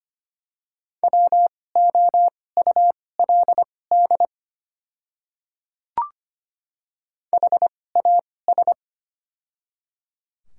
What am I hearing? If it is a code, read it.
Morse code, heard as a keyed tone, about 25 wpm, spaced as for 18 wpm, 700 Hz. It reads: WOULD HAS
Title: 2x Character Spacing - Top 100 Words - 25wpm